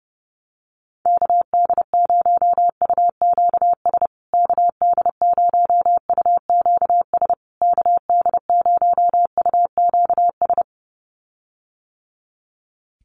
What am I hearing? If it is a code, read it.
Morse code, heard as a keyed tone, about 30 wpm, 700 Hz. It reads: KD0UQS KD0UQS KD0UQS
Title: USA Call Signs Repeated 3x 30wpm